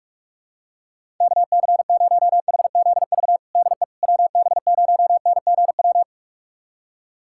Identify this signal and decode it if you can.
Morse code, heard as a keyed tone, about 46 wpm, 700 Hz. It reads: KC0HZV DE WB0NGW
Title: USA Callsign DE Callsign - 6 Characters - 45wpm